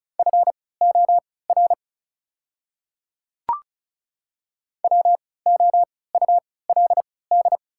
Morse code, heard as a keyed tone, about 36 wpm, spaced as for 20 wpm, 700 Hz. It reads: FOR WOULD
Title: 3x Character Spacing - Top 100 Words - 35wpm